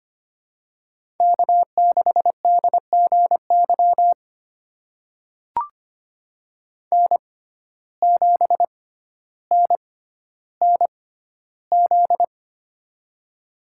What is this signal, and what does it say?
Morse code, heard as a keyed tone, about 25 wpm, spaced as for 13 wpm, 700 Hz. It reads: K6DGY N7NNZ
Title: USA Call Signs - 5 Characters - 6x Character Spacing - 25wpm